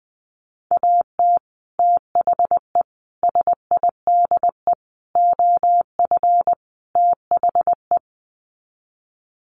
Morse code, heard as a keyed tone, about 20 wpm, 700 Hz. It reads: AT THE SIDE OF THE
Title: Sets of 5 Words Encore2 20wpm